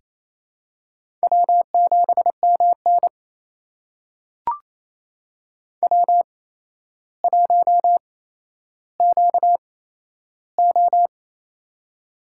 Morse code, heard as a keyed tone, about 28 wpm, spaced as for 12 wpm, 700 Hz. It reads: W7MN W1QO
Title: USA Call Signs - 4 Characters - 8x Character Spacing - 28wpm